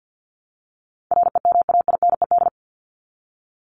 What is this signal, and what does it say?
Morse code, heard as a keyed tone, about 50 wpm, 700 Hz. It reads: REMAINED
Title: ED Suffix - Top 100 Words with Base Words 50wpm